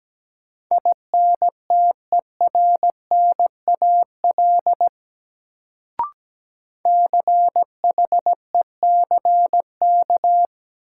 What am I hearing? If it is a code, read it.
Morse code, heard as a keyed tone, about 17 wpm, 700 Hz. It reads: INTERNAL CHECK
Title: Sentences from Top 1400 Words - New Word Drill Alt - 17wpm